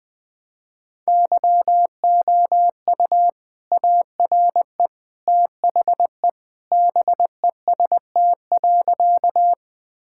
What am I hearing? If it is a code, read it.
Morse code, heard as a keyed tone, about 20 wpm, 700 Hz. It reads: YOU ARE THE BEST.
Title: Sentences with 2-Word Phrases - 20wpm